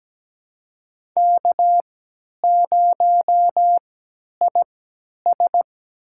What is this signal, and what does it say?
Morse code, heard as a keyed tone, about 17 wpm, spaced as for 10 wpm, 700 Hz. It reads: K0IS
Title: USA Call Signs - 3x Character Spacing - 17wpm